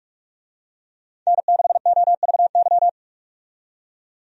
Morse code, heard as a keyed tone, about 41 wpm, 700 Hz. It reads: N6OVY
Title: USA Call Signs - 5 Characters - 45wpm